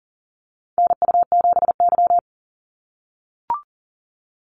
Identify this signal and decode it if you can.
Morse code, heard as a keyed tone, about 40 wpm, 700 Hz. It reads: NU7Y
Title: NA SOTA Chaser Callsigns 40wpm